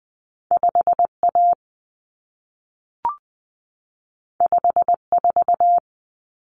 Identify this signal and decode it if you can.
Morse code, heard as a keyed tone, about 20 wpm, 700 Hz. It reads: 5A 54
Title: Common Cut Numbers - 2-Digit 20wpm